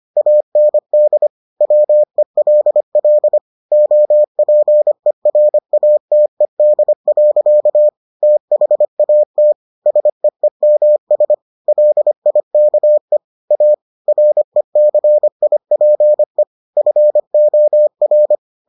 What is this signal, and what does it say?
Morse code, heard as a keyed tone, about 25 wpm, 600 Hz. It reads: AND WELL OPERATED. THAT SEEMS LIKE A RECIPE FOR